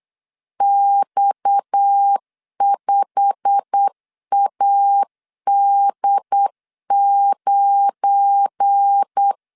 Morse code, heard as a keyed tone, about 8 wpm, 800 Hz. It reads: X5AD9
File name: LEZIONI / 8 YJ45 / 8 YJ45 8wpm 3214